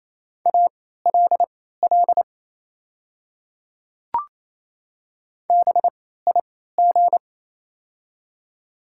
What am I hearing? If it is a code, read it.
Morse code, heard as a keyed tone, about 28 wpm, spaced as for 16 wpm, 700 Hz. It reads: ALL BIG
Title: 3x Character Spacing - Top 200 Words - 28wpm